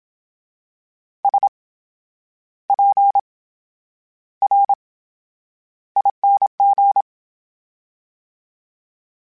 Morse code, heard as a keyed tone, about 27 wpm, spaced as for 9 wpm, 800 Hz. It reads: SPRING
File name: Special INGs / Special INGs-mp3s / VerySlowToFaster Ing @25